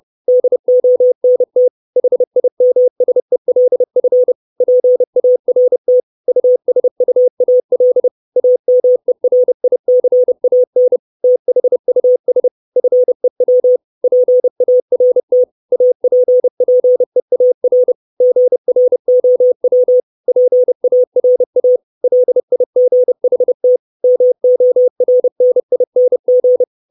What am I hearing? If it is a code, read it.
Morse code, heard as a keyed tone, about 30 wpm, 500 Hz. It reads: DONT HIMSELF PART USUAL AMERICAN THUS FEW PART APPEAR GROW PARA LIGHT MORNING
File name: 632 words / 632 Words 30WPM